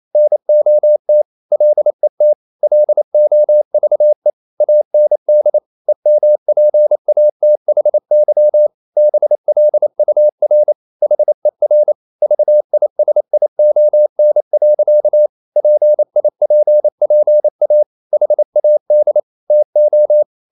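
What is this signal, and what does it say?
Morse code, heard as a keyed tone, about 28 wpm, 600 Hz. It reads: NOT LET LOVE AND EMPATHY BLUR HER VISION. PIPPA HAD TO
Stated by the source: Morse code practice at 28 wpm, text of Ken Follett's novel Never in English, from CW med Gnister.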